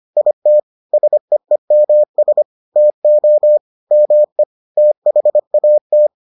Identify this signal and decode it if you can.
Morse code, heard as a keyed tone, about 25 wpm, 600 Hz. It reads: IT SEEMS TO ME THAT